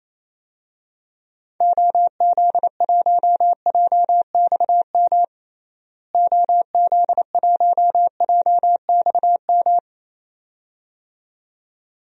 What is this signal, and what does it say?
Morse code, heard as a keyed tone, about 28 wpm, 700 Hz. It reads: OZ1JXM OZ1JXM
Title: DXCC-100 Call Signs - 6 Characters - Repeated 2x - 28wpm